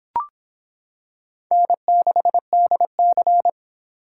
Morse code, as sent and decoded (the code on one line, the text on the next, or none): -. -.... -.. -.-.
N6DC